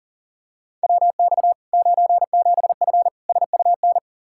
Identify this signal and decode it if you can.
Morse code, heard as a keyed tone, about 40 wpm, 700 Hz. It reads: WX 97F SUN
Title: Practice QSO - Exchange 2 - Info - 40wpm